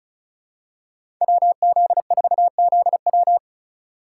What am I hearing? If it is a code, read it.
Morse code, heard as a keyed tone, about 36 wpm, 700 Hz. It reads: WZ4ZW
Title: USA Call Signs - 5 Characters - 3x Character Spacing - 35wpm